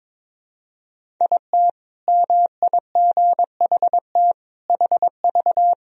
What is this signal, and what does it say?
Morse code, heard as a keyed tone, about 22 wpm, 700 Hz. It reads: IT MIGHT HV
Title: Sentences from Top 500 Words and Abbreviations 22wpm